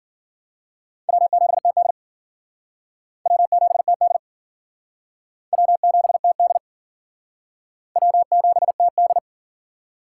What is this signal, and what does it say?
Morse code, heard as a keyed tone, about 53 wpm, 700 Hz. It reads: W7TD W7TD W7TD W7TD